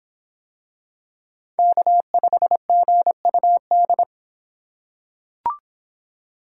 Morse code, heard as a keyed tone, about 26 wpm, 700 Hz. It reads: K5GUD